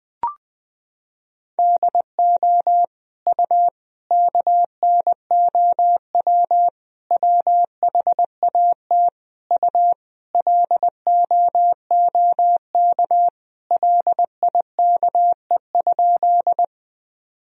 Morse code, heard as a keyed tone, about 20 wpm, 700 Hz. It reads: DO U KNOW WHAT U LOOK LIKE?
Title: Sentences from Top 100 Words and Abbreviations 20wpm